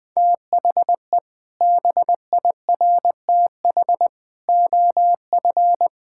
Morse code, heard as a keyed tone, about 21 wpm, 700 Hz. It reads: THE BIRTH OF
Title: Sentences from Top 1000 Words - Review Part 5 - 20wpm